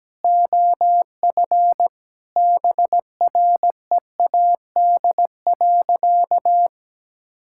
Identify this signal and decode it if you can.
Morse code, heard as a keyed tone, about 17 wpm, 700 Hz. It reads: OF BREAD.
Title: Sentences from Top 1600 Words 17wpm